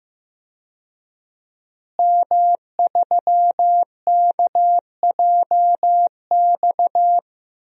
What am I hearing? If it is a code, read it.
Morse code, heard as a keyed tone, about 15 wpm, 700 Hz. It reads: M3KJX
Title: DXCC-100 Call Signs - 5 Characters - Repeated 3x - 15wpm